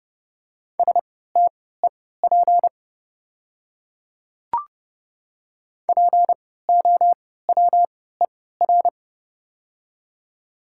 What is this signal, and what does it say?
Morse code, heard as a keyed tone, about 30 wpm, spaced as for 17 wpm, 700 Hz. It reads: STEP POWER
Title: 3x Character Spacing - Top 500 Words - 30wpm